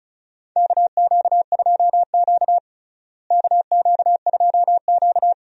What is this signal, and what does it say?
Morse code, heard as a keyed tone, about 36 wpm, 700 Hz. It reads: KQ2Q KQ2Q
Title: USA Call Signs - 4 Characters - Repeated 2x - 35wpm